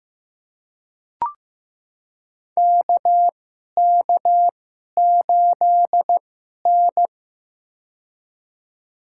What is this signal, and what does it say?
Morse code, heard as a keyed tone, about 15 wpm, spaced as for 11 wpm, 690 Hz. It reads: KK8N
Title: USA Call Signs - 4 Characters - 2x Character Spacing - 15wpm